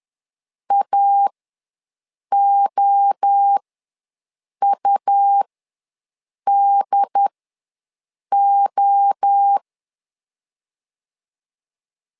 Morse code, heard as a keyed tone, about 10 wpm, spaced as for 6 wpm, 800 Hz. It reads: AOUDO
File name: LEZIONI / 3 MWRO / 3 MWRO ws 10wpm 2310